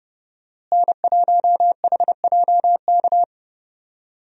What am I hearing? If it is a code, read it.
Morse code, heard as a keyed tone, about 31 wpm, 700 Hz. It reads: N1HJK
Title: USA Call Signs - 5 Characters - Repeated 2x - 30wpm